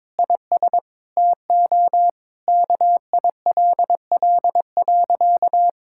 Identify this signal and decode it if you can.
Morse code, heard as a keyed tone, about 21 wpm, 700 Hz. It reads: IS TO KILL.
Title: Sentences from Top 1600 Words 22wpm